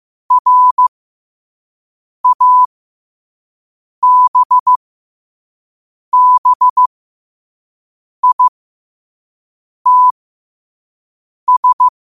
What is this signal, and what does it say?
Morse code, heard as a keyed tone, about 15 wpm, spaced as for 5 wpm, 1000 Hz. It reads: RABBITS
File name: qotdfiles / 5wpm